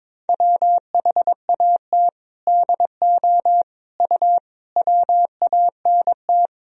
Morse code, heard as a keyed tone, about 21 wpm, 700 Hz. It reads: WHAT DO U WANT
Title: Sentences from Top 200 Words and Abbreviations 22wpm